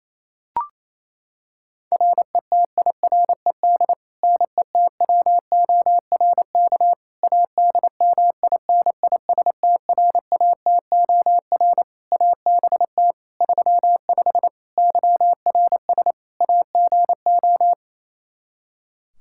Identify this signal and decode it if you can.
Morse code, heard as a keyed tone, about 27 wpm, 700 Hz. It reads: RETIRED NETWORK ADMINISTRATOR ABT 35 YRS AGO
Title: Practice QSO - Exchange 3 - More Info - 28wpm